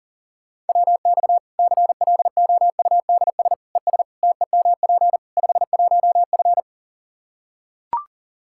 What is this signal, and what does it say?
Morse code, heard as a keyed tone, about 37 wpm, 700 Hz. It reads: WX CLOUDS ES TEMP 51F
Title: Practice QSO - Exchange 2 - Info - 40wpm